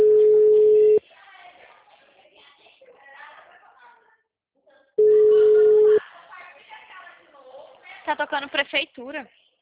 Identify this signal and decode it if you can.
Ringback tone (Europe, 425 Hz), with speech.